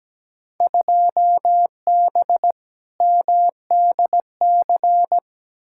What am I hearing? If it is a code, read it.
Morse code, heard as a keyed tone, about 17 wpm, 700 Hz. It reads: 2B MDC
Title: Field Day Exchanges - 17wpm